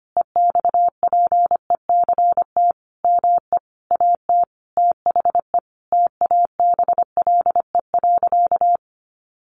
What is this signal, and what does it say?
Morse code, heard as a keyed tone, about 25 wpm, 700 Hz. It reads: EXPECT ME AT THE TABLE.
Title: Sentences from Top 1100 Words 25wpm